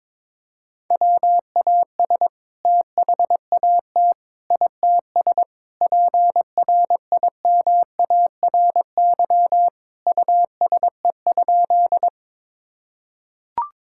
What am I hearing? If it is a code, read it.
Morse code, heard as a keyed tone, about 22 wpm, 700 Hz. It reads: WAS THAT ITS PRIMARY USE?
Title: Sentences from Top 1000 Words - Review Part 5 - 22wpm